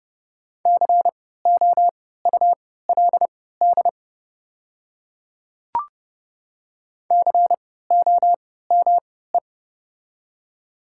Morse code, heard as a keyed tone, about 30 wpm, spaced as for 17 wpm, 700 Hz. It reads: COULD COME
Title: 3x Character Spacing - Top 100 Words - 30wpm